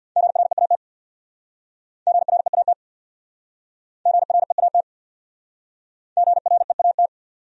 Morse code, heard as a keyed tone, about 66 wpm, 700 Hz. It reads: GREAT GREAT GREAT GREAT